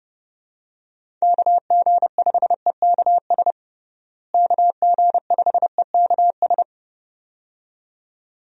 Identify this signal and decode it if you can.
Morse code, heard as a keyed tone, about 30 wpm, 700 Hz. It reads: KG5EKS KG5EKS
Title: USA Call Signs - 6 Characters - Repeated 2x - 30wpm